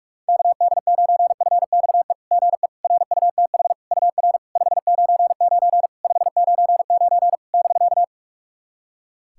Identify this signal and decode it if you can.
Morse code, heard as a keyed tone, about 48 wpm, 700 Hz. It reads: KD9FXE GE RUTH UR 599 599 <BK>
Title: SOTA QSO Practice 45wpm